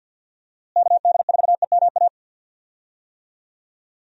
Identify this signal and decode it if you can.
Morse code, heard as a keyed tone, about 47 wpm, 700 Hz. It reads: KD4EMA
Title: USA Call Signs - 6 Characters - Repeated 2x - 50wpm